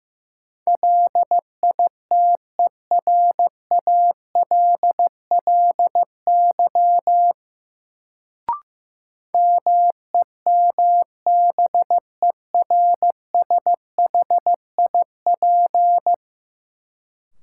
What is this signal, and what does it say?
Morse code, heard as a keyed tone, about 15 wpm, 700 Hz. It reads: LITERALLY MEMBERSHIP
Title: Sentences from Top 1400 Words - New Word Drill Alt - 15wpm